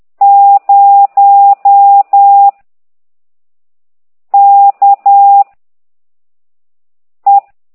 Morse code, heard as a keyed tone, about 10 wpm, 800 Hz. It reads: 0 K E
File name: ct2idw / exerc15